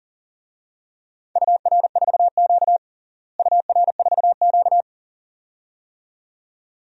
Morse code, heard as a keyed tone, about 40 wpm, 700 Hz. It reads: UR4Q UR4Q
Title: DXCC-100 Call Signs - 4 Characters - Repeated 2x - 40wpm